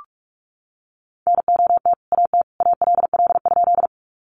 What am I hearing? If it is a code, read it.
Morse code, heard as a keyed tone, about 44 wpm, 700 Hz. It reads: NOT AT ALL?
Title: Sentences from Top 600 Words 45wpm